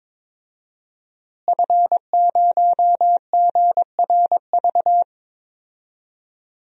Morse code, heard as a keyed tone, about 21 wpm, 700 Hz. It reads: F0GRV
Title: DXCC-100 Call Signs - 5 Characters - Repeated 2x - 22wpm